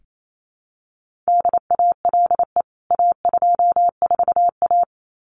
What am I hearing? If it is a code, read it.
Morse code, heard as a keyed tone, about 28 wpm, 700 Hz. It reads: DALE A24A
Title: CWT Exchange 28wpm